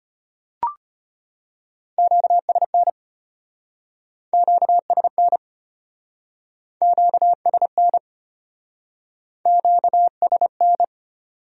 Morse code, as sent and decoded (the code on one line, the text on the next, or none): --.- ... -. / --.- ... -. / --.- ... -. / --.- ... -.
QSN QSN QSN QSN